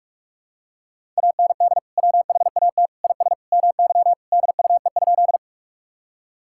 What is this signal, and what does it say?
Morse code, heard as a keyed tone, about 46 wpm, 700 Hz. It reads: AND WHAT IS MY DUE?
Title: Sentences from Top 1000 Words - Review Part 6 - 45wpm